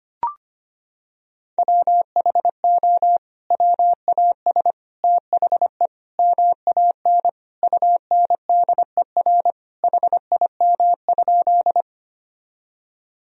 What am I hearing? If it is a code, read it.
Morse code, heard as a keyed tone, about 25 wpm, 700 Hz. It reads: WHO WAS THE MAN UNDER HIM?
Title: Sentences from Top 1000 Words - Review Part 1 - 25wpm